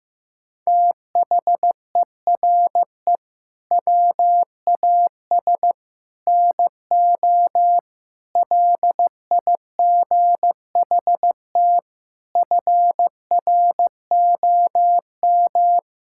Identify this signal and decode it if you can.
Morse code, heard as a keyed tone, about 14 wpm, 700 Hz. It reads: THERE WAS NO LIGHT FROM
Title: Sentences from Top 1000 Words - Review Part 5 - 15wpm